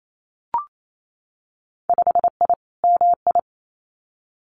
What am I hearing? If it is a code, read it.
Morse code, heard as a keyed tone, about 28 wpm, 700 Hz. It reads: 5I MI
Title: Winter Field Day Exchanges Encore 28wpm